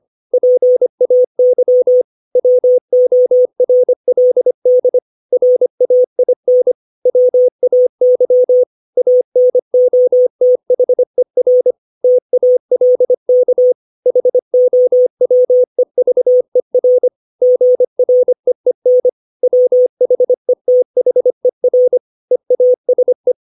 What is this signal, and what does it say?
Morse code, heard as a keyed tone, about 25 wpm, 500 Hz. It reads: PAY WORLD RAIN WAY ANOTHER TALK HOWEVER GREEN WHETHER EASE
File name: 632 words / 632 Words 25WPM